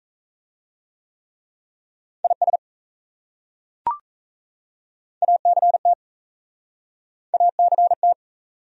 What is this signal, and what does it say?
Morse code, heard as a keyed tone, about 37 wpm, 700 Hz. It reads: IS ACT ACT